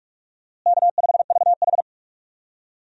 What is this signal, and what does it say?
Morse code, heard as a keyed tone, about 41 wpm, 700 Hz. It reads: K5VH
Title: USA Call Signs - 4 Characters - Repeated 3x - 45wpm